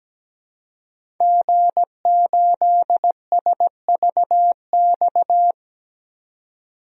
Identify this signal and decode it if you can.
Morse code, heard as a keyed tone, about 17 wpm, 700 Hz. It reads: G8SVX
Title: DXCC-100 Call Signs - 5 Characters - 17wpm